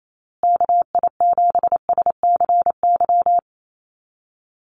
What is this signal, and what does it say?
Morse code, heard as a keyed tone, about 28 wpm, 700 Hz. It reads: KI7SCY
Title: USA Call Signs Repeated 3x 28wpm